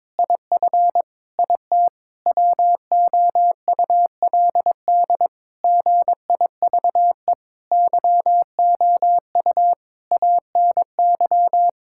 Morse code, heard as a keyed tone, about 22 wpm, 700 Hz. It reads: IF IT WOULD GIVE YOU ANY
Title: Sentences from Top 1600 Words 22wpm